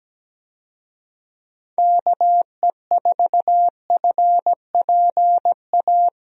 Morse code, heard as a keyed tone, about 17 wpm, 700 Hz. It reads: KE4FPA
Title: USA Call Signs - 6 Characters - Repeated 3x - 17wpm